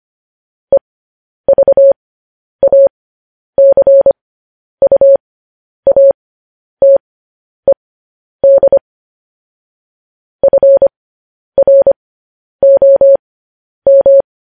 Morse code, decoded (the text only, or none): EVACUATED FROM